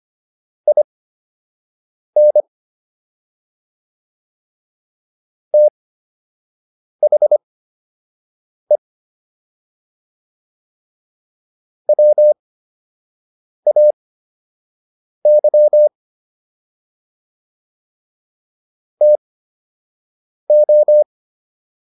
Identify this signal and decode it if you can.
Morse code, heard as a keyed tone, about 25 wpm, spaced as for 6 wpm, 600 Hz. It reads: IN THE WAY TO